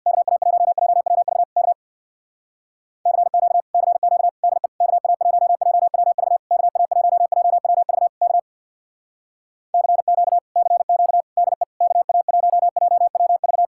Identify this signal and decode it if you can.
Morse code, heard as a keyed tone, about 62 wpm, 700 Hz. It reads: KA1JWV K CQ CQ DE KA1JWV KA1JWV K CQ CQ DE KA1JWV
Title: Speed-Racing Common QSO Phrases - 50wpm